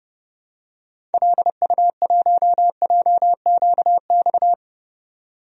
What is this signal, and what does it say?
Morse code, heard as a keyed tone, about 30 wpm, 700 Hz. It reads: LU1JQX
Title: DXCC-100 Call Signs - 6 Characters - 30wpm